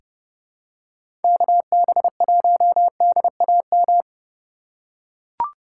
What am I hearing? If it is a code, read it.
Morse code, heard as a keyed tone, about 30 wpm, 700 Hz. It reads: KB1DAM
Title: USA Call Signs - 6 Characters - 2x Character Spacing - 30wpm